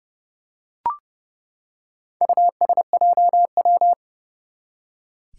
Morse code, heard as a keyed tone, about 30 wpm, 700 Hz. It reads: USJW
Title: Four Challenging Letter-Number Combinations 30wpm